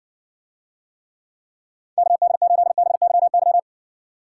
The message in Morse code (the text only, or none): KD8BYY